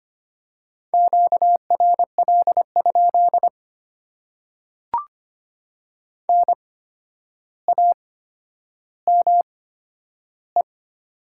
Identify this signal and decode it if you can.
Morse code, heard as a keyed tone, about 25 wpm, spaced as for 11 wpm, 700 Hz. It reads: QRL? NAME
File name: Top 90 QSO Elements - 8x Character Spacing - 25wpm